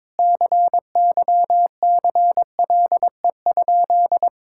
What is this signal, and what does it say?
Morse code, heard as a keyed tone, about 22 wpm, 700 Hz. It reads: CYCLE?
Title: Sentences from Top 1000 Words - Review Part 6 - 22wpm